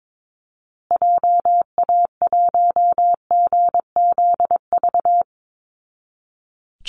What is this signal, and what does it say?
Morse code, heard as a keyed tone, about 22 wpm, 700 Hz. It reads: JA1GZV
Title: Call Signs Repeated 2x - 22wpm